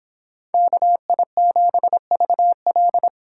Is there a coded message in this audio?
Yes: Morse code from 0.5 to 3.1 s, heard as a keyed tone, about 30 wpm, 700 Hz. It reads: KI7VL